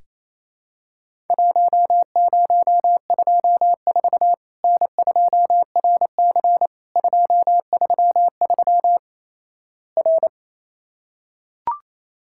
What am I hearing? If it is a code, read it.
Morse code, heard as a keyed tone, about 28 wpm, 700 Hz. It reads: 1024 N2RC 233 R
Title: Worked All Europe DX Contest - QTC Exchange - 28wpm